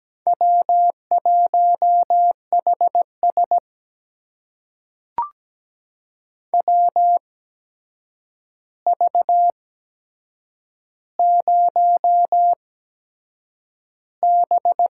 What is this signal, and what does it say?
Morse code, heard as a keyed tone, about 17 wpm, spaced as for 7 wpm, 700 Hz. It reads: W1HS WV0B
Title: USA Call Signs - 4 Characters - 8x Character Spacing - 17wpm